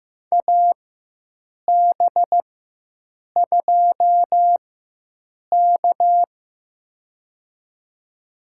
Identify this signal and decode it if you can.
Morse code, heard as a keyed tone, about 15 wpm, spaced as for 7 wpm, 700 Hz. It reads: AB2K
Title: USA Call Signs - 4 Characters - 4x Character Spacing - 15wpm